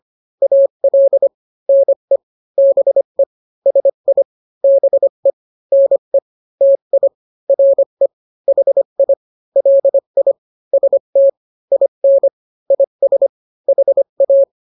Morse code, heard as a keyed tone, about 25 wpm, 550 Hz. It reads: AL NE BE SI BE NE TI RE HI LI ST IN IS HA